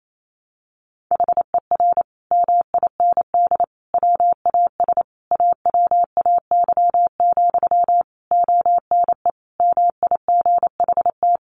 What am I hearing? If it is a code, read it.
Morse code, heard as a keyed tone, about 28 wpm, 700 Hz. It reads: HER MIND WAS AWAY, ONE MIGHT
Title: Sentences from Top 1500 Words 28wpm